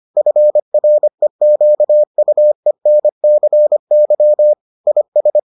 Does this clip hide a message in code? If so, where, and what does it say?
Yes: Morse code from 0.2 to 5.4 s, heard as a keyed tone, about 25 wpm, 600 Hz. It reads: FREQUENCY IS